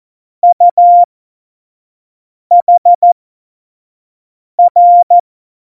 Morse code, heard as a keyed tone, about 14 wpm, spaced as for 5 wpm, 700 Hz. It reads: UHR
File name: Morsekurs / Witze / Witz 12. September 17, Tempo 25